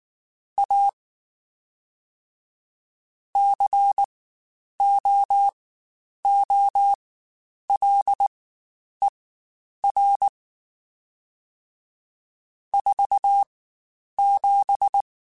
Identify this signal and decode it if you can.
Morse code, heard as a keyed tone, about 19 wpm, spaced as for 9 wpm, 800 Hz. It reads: A COOLER 47
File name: Spaced Characters / Spaced Characters-mp3s / Weather5 5 sp @18